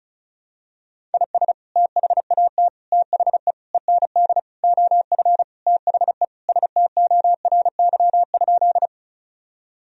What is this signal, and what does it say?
Morse code, heard as a keyed tone, about 34 wpm, 700 Hz. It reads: IS THAT THE END OF THE STORY?
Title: Sentences from Top 1000 Words - Review Part 1 - 35wpm